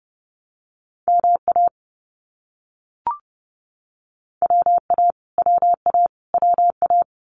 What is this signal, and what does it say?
Morse code, heard as a keyed tone, about 30 wpm, 700 Hz. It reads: MA WA WA WA
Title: US States Repeated 3x 30wpm